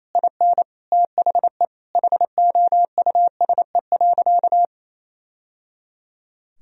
Morse code, heard as a keyed tone, about 28 wpm, 700 Hz. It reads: IN THE HOUSE.
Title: Sentences from Top 1000 Words - Review Part 4 - 28wpm